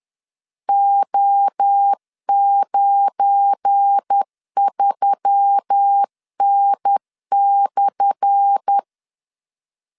Morse code, heard as a keyed tone, about 10 wpm, 800 Hz. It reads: O93N/
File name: LEZIONI / 10 SIGNS / 10A SLASH 10wpm 2808